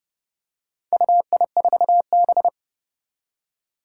Morse code, heard as a keyed tone, about 28 wpm, 700 Hz. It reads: UI4B
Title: Four Challenging Letter-Number Combinations 30wpm